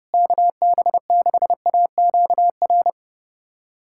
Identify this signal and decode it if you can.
Morse code, heard as a keyed tone, about 30 wpm, 700 Hz. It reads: KB6AQR
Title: USA Callsign DE Callsign 30wpm